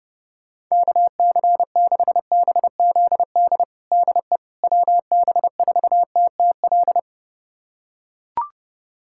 Morse code, heard as a keyed tone, about 30 wpm, 700 Hz. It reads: KC6BZD DE WB4TTL
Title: USA Callsign DE Callsign - 6 Characters - 30wpm